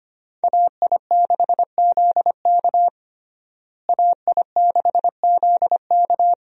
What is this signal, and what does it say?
Morse code, heard as a keyed tone, about 25 wpm, 700 Hz. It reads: AI6ZK AI6ZK
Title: USA Call Signs - 5 Characters - Repeated 3x - 25wpm